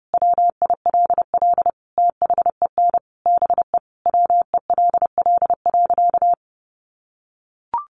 Morse code, heard as a keyed tone, about 30 wpm, 700 Hz. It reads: WILL THEN BE WELL.
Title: Sentences from Top 1500 Words 30wpm